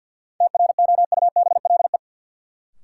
Morse code, heard as a keyed tone, about 53 wpm, 700 Hz. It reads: TROUBLE
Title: Sentences from Top 1200 Words - New Word Drill - 50wpm